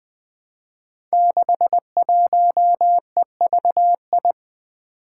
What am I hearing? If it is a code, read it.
Morse code, heard as a keyed tone, about 20 wpm, 700 Hz. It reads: 61EVI
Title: Five Challenging Letter-Number Combinations 20wpm